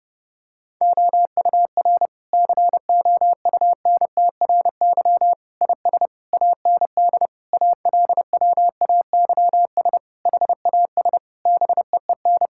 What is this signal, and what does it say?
Morse code, heard as a keyed tone, about 30 wpm, 700 Hz. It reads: OUR COUNTRY IS AND ALWAYS HAS BEEN
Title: Sentences from Top 1000 Words - Review Part 6 - 30wpm